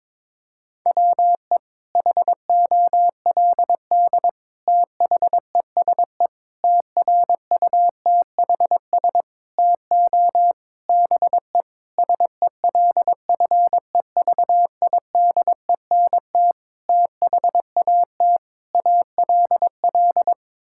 Morse code, heard as a keyed tone, about 22 wpm, 700 Hz. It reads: WE HOLD THESE TRUTHS TO BE SELFEVIDENT THAT ALL
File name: Sets of 9 Words - 22wpm